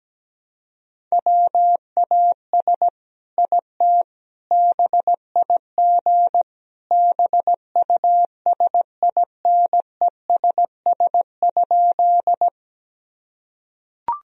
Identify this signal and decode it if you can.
Morse code, heard as a keyed tone, about 17 wpm, 700 Hz. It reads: WAS IT BIG BUSINESS?
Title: Sentences from Top 1000 Words - Review Part 1 - 17wpm